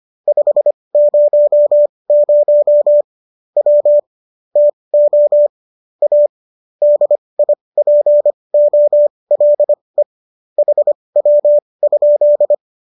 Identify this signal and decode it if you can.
Morse code, heard as a keyed tone, about 25 wpm, spaced as for 20 wpm, 590 Hz. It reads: 500 W TO A DIPOLE HW?